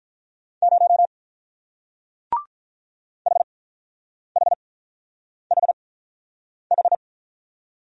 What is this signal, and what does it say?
Morse code, heard as a keyed tone, about 45 wpm, spaced as for 9 wpm, 700 Hz. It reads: 0 HHHH